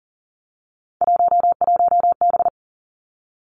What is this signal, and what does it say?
Morse code, heard as a keyed tone, about 40 wpm, 700 Hz. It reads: 11B
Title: Three Challenging Letter-Number Combinations 40wpm